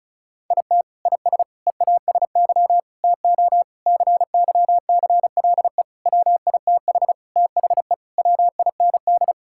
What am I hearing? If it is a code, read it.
Morse code, heard as a keyed tone, about 34 wpm, 700 Hz. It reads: IT IS EASY TO CYCLE WITH THE WIND
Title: Sentences from Top 1000 Words - Review Part 6 - 35wpm